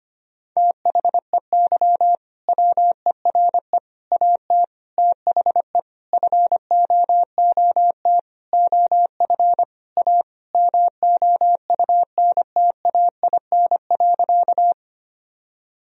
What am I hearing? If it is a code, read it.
Morse code, heard as a keyed tone, about 25 wpm, 700 Hz. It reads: THEY WERE AT THE FOOT OF A MOUNTAIN.
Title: Sentences from Top 1000 Words - Review Part 3 - 25wpm